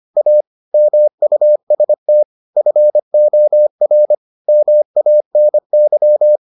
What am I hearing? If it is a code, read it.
Morse code, heard as a keyed tone, about 25 wpm, 600 Hz. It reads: A MUST FOR MANY